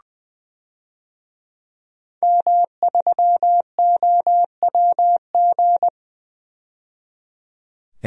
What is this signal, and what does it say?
Morse code, heard as a keyed tone, about 20 wpm, 700 Hz. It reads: M3OWG
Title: DXCC-100 Call Signs - 5 Characters - 20wpm